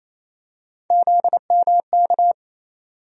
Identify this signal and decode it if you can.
Morse code, heard as a keyed tone, about 28 wpm, 700 Hz. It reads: ZMK
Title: USA Call Sign Postfixes - 3 Letters 28wpm